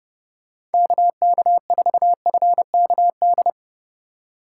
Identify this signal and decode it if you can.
Morse code, heard as a keyed tone, about 30 wpm, 700 Hz. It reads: KK4FKD
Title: USA Call Signs - 4x Character Spacing - 30wpm